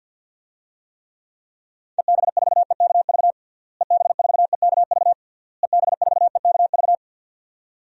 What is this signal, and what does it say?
Morse code, heard as a keyed tone, about 49 wpm, 700 Hz. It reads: EB4EKV EB4EKV EB4EKV
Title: DXCC-100 Call Signs - 6 Characters - Repeated 3x - 50wpm